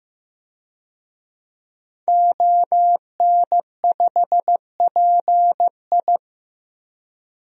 Morse code, heard as a keyed tone, about 15 wpm, 700 Hz. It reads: ON5PI
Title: DXCC-100 Call Signs - 5 Characters - Repeated 3x - 15wpm